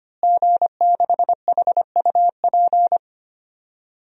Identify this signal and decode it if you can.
Morse code, heard as a keyed tone, about 25 wpm, 700 Hz. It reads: G6HUP
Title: DXCC-100 Call Signs - 5 Characters - Repeated 2x - 25wpm